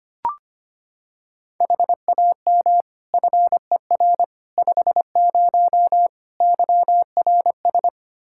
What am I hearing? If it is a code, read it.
Morse code, heard as a keyed tone, about 25 wpm, 700 Hz. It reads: HAM FER 50 YRS
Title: Practice QSO - Exchange 3 - More Info - 25wpm